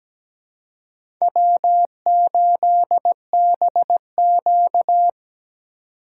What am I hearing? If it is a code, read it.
Morse code, heard as a keyed tone, about 17 wpm, 700 Hz. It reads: W8BQ
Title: USA Call Signs - 4 Characters - 3x Character Spacing - 17wpm